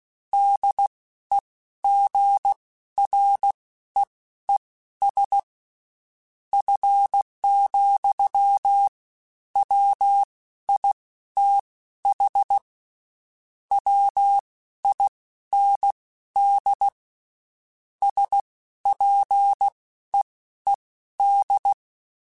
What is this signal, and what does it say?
Morse code, heard as a keyed tone, about 16 wpm, spaced as for 11 wpm, 800 Hz. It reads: DEGREES F,WITH WIND SPEED
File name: Spaced Characters / Spaced Characters-mp3s / Weather 5 5 sp @15